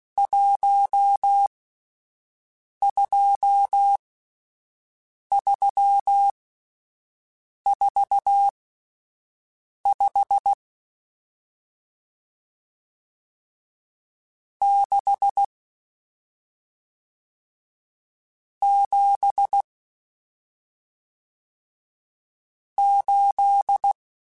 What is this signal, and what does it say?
Morse code, heard as a keyed tone, about 16 wpm, spaced as for 5 wpm, 800 Hz. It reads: 12345 6 7 8